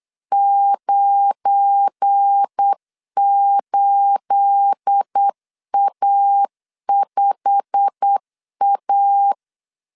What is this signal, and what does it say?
Morse code, heard as a keyed tone, about 8 wpm, 800 Hz. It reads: 98A5A
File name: CW / DET / 1 9 1500C S B 8wpm 5624